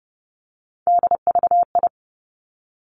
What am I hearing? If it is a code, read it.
Morse code, heard as a keyed tone, about 30 wpm, 700 Hz. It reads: DVI
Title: Three Challenging Letter-Number Combinations 30wpm